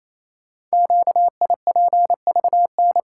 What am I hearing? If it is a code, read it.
Morse code, heard as a keyed tone, about 28 wpm, 700 Hz. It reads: QIPVN
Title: Code Groups - 5 Letter:Numbers - 28wpm